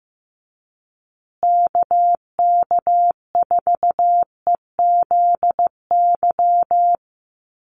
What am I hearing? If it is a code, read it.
Morse code, heard as a keyed tone, about 15 wpm, 700 Hz. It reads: KK4EZY
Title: USA Call Signs Repeated 2x 15wpm